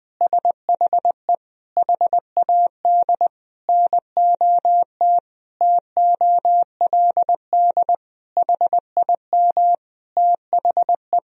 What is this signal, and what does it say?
Morse code, heard as a keyed tone, about 20 wpm, 700 Hz. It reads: SHE HAD NOT TOLD HIM THE
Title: Sentences from Top 1000 Words - Review Part 1 - 20wpm